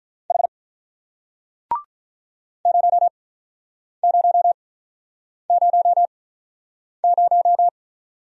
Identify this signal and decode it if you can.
Morse code, heard as a keyed tone, about 45 wpm, spaced as for 9 wpm, 700 Hz. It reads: H 0000